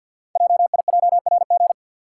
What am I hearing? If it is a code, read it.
Morse code, heard as a keyed tone, about 52 wpm, 700 Hz. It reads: JIJRG
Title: Code Groups - 5 Letters - 50wpm